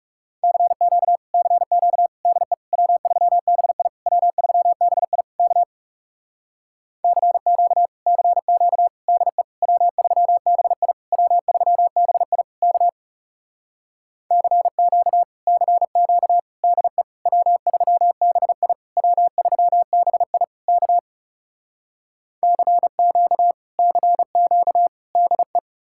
Morse code, heard as a keyed tone, about 40 wpm, 700 Hz. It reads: CQ CQ DE W3BI W3BI K CQ CQ DE W3BI W3BI K CQ CQ DE W3BI W3BI K CQ CQ DE